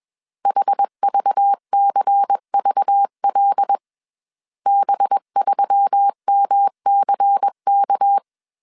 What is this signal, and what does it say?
Morse code, heard as a keyed tone, about 21 wpm, 800 Hz. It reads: HVCVL B3MCK
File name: CW / DET / 1 6 1500C S A 20wpm 2048